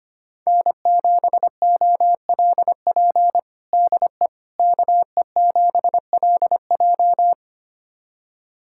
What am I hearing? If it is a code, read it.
Morse code, heard as a keyed tone, about 25 wpm, 700 Hz. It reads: N7OLP DE KE7LJ
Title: USA Callsign DE Callsign - 5 Characters - 25wpm